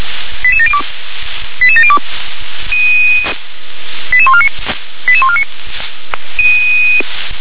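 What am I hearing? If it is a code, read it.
Five-tone selective call (ZVEI), heard as quick runs of five tones, tones of about 70 ms. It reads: 88702 88702 88149 88149
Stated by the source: Five-tone ZVEI selective call 88702 received on SDR.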